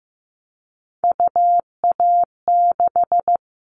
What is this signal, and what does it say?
Morse code, heard as a keyed tone, about 15 wpm, 700 Hz. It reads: UA6
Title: Callsign Prefixes for Top 50 DXCC - 15wpm